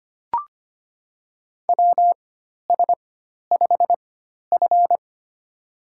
Morse code, heard as a keyed tone, about 25 wpm, spaced as for 12 wpm, 690 Hz. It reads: WS5F
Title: USA Call Signs - 4 Characters - 4x Character Spacing - 25wpm